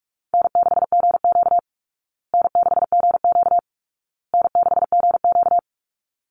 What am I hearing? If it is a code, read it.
Morse code, heard as a keyed tone, about 44 wpm, 700 Hz. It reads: N6GQ N6GQ N6GQ
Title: Easy Call Signs Repeated 3x - 45wpm